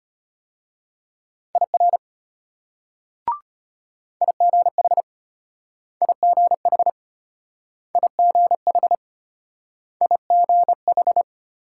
Morse code, heard as a keyed tone, about 35 wpm, 700 Hz. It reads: IR IGH IGH IGH IGH